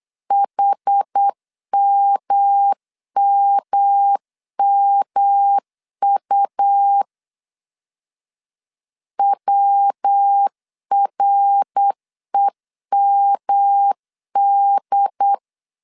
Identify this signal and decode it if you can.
Morse code, heard as a keyed tone, about 8 wpm, 800 Hz. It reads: HMMMU WREMD
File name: CW / DET / 1 3 1500C S B 8wpm 4141